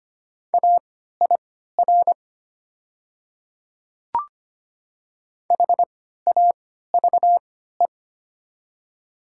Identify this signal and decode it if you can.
Morse code, heard as a keyed tone, about 25 wpm, spaced as for 14 wpm, 700 Hz. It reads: AIR HAVE
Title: 3x Character Spacing - Top 500 Words - 25wpm